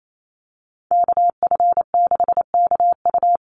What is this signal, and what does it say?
Morse code, heard as a keyed tone, about 28 wpm, 700 Hz. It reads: KF6KU
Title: USA Call Signs 28wpm